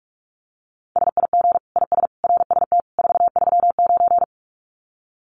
Morse code, heard as a keyed tone, about 44 wpm, 700 Hz. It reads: SIG IS RST 439
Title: Common QSO Phrases 45wpm